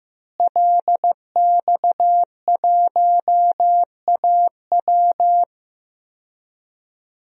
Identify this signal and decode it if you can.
Morse code, heard as a keyed tone, about 15 wpm, 700 Hz. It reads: LX1AW
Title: DXCC-100 Call Signs - 5 Characters - Repeated 3x - 15wpm